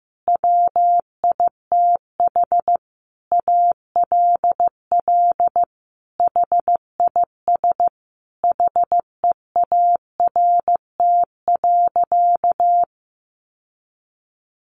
Morse code, heard as a keyed tone, about 15 wpm, 700 Hz. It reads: WITH ALL HIS HEART.
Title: Sentences from Top 1200 Words 15wpm